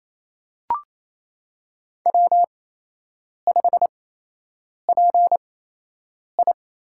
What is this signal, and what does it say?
Morse code, heard as a keyed tone, about 28 wpm, spaced as for 8 wpm, 700 Hz. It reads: W5PI
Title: USA Call Signs - 8x Character Spacing - 28wpm